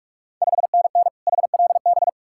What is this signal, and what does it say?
Morse code, heard as a keyed tone, about 44 wpm, 690 Hz. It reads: 5NN HLB
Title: Ontario QSO Party Exchange 45wpm